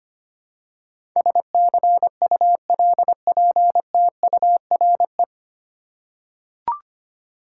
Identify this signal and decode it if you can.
Morse code, heard as a keyed tone, about 25 wpm, 700 Hz. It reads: SCULPTURE
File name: Nine Letter Words - 25wpm